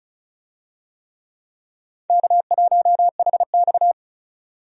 Morse code, heard as a keyed tone, about 34 wpm, 700 Hz. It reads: K1HX
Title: USA Call Signs - 4 Characters - 35wpm